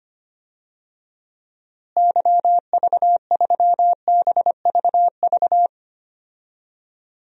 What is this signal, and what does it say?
Morse code, heard as a keyed tone, about 25 wpm, 700 Hz. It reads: YV3BVV
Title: DXCC-100 Call Signs - 6 Characters - Repeated 3x - 25wpm